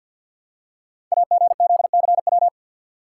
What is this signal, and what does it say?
Morse code, heard as a keyed tone, about 50 wpm, 700 Hz. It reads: AGZCW
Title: Code Groups - 5 Letters - 50wpm